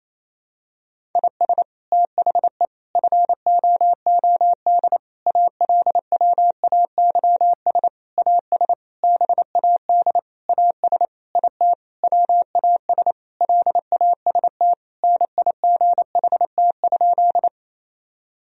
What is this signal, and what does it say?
Morse code, heard as a keyed tone, about 28 wpm, 700 Hz. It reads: IS THE FOOD ALWAYS AS BAD AS IT WAS LAST NIGHT?
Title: Sentences from Top 1000 Words - Review Part 1 - 28wpm